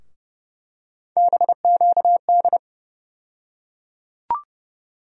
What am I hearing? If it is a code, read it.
Morse code, heard as a keyed tone, about 31 wpm, 700 Hz. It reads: BQD
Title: USA Call Sign Postfixes - 3 Letters 30wpm